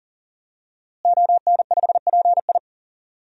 Morse code, heard as a keyed tone, about 40 wpm, 700 Hz. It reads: ON5PI
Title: DXCC-100 Call Signs - 5 Characters - Repeated 3x - 40wpm